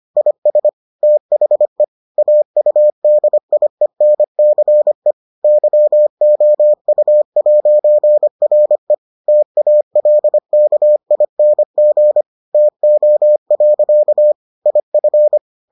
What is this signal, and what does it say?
Morse code, heard as a keyed tone, about 25 wpm, 600 Hz. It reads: IS THE AUDIENCE YOU'RE TALKING TO. IF